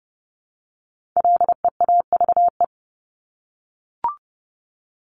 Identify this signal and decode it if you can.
Morse code, heard as a keyed tone, about 30 wpm, 700 Hz. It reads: LEAVE
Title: Top 500 Words Repeated 30wpm